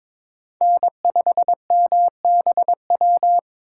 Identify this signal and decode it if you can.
Morse code, heard as a keyed tone, about 22 wpm, 700 Hz. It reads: N5MBW
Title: USA Call Signs - 5 Characters - 4x Character Spacing - 22wpm